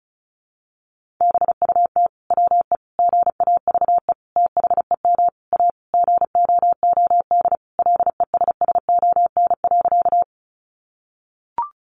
Morse code, heard as a keyed tone, about 34 wpm, 700 Hz. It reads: BUT WE GAVE THEM A GOOD LESSON.
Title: Sentences from Top 1200 Words 35wpm